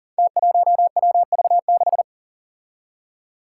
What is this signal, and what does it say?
Morse code, heard as a keyed tone, about 40 wpm, 700 Hz. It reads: T1WV6
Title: Five Challenging Letter-Number Combinations 40wpm